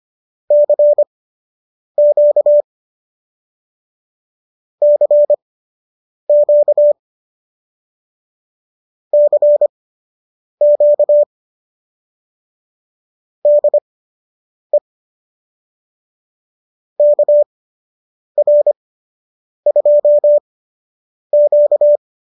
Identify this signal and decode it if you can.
Morse code, heard as a keyed tone, about 25 wpm, spaced as for 8 wpm, 590 Hz. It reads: CQ CQ CQ DE KR2Q